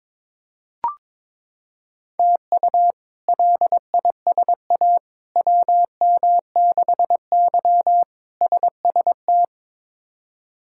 Morse code, heard as a keyed tone, about 22 wpm, 700 Hz. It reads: TU LISA WM6Y SST
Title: K1USN SST QSO Practice - 22wpm